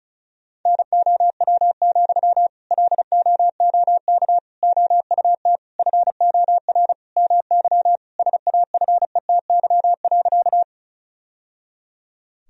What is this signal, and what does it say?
Morse code, heard as a keyed tone, about 36 wpm, 700 Hz. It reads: NOW, LOOK OUT FOR MY SAFETY.
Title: Sentences from Top 1000 Words - Review Part 4 - 35wpm